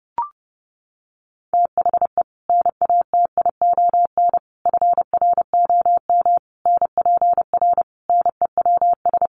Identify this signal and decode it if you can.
Morse code, heard as a keyed tone, about 30 wpm, 700 Hz. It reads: THE NATION FROM NPR NEWS
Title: Sets of 5 Words 30wpm